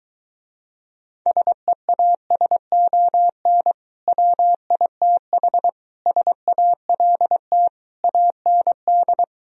Morse code, heard as a keyed tone, about 22 wpm, 700 Hz. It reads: SEASON WITH SALT AND